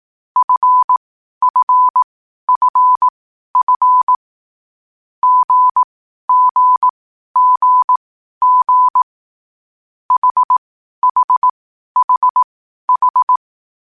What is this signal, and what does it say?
Morse code, heard as a keyed tone, about 18 wpm, spaced as for 12 wpm, 1000 Hz. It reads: FFFF GGGG HHHH
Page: Hier findet Ihr das Morsealphabet in langsamer Geschwindigkeit zum Hören. Dabei ist jeder Buchstabe vier Mal hintereinander aufgenommen worden, danach kommt direkt der nächste.